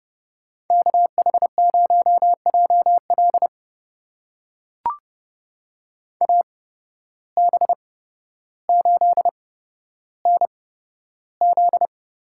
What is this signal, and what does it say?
Morse code, heard as a keyed tone, about 31 wpm, spaced as for 13 wpm, 700 Hz. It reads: KH0JL AB8NZ
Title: USA Call Signs - 5 Characters - 8x Character Spacing - 30wpm